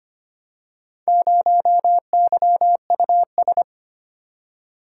Morse code, heard as a keyed tone, about 25 wpm, 700 Hz. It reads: 0YUS
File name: practice / 25wpm / Four Letter-Number Combinations - 25wpm